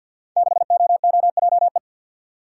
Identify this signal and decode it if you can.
Morse code, heard as a keyed tone, about 51 wpm, 700 Hz. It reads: 6OOJE
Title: Code Groups - 5 Letter:Numbers - 50wpm